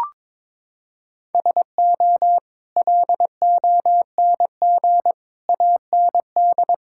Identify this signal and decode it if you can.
Morse code, heard as a keyed tone, about 22 wpm, 700 Hz. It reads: SO LONG AND
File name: practice / 22wpm / Sets of 8 Words - 22wpm